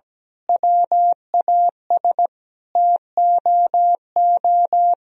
Morse code, heard as a keyed tone, about 17 wpm, 700 Hz. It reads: WAS TOO
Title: Sentences from Top 1000 Words - Review Part 1 - 17wpm